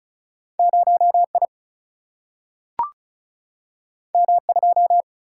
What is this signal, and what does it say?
Morse code, heard as a keyed tone, about 36 wpm, 700 Hz. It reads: 0 I M 2
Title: Single-Letter Single-Number 35wpm